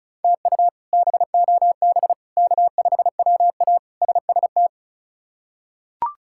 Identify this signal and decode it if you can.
Morse code, heard as a keyed tone, about 36 wpm, 700 Hz. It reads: TU BOB K5WA SST
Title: K1USN SST QSO Practice - 35wpm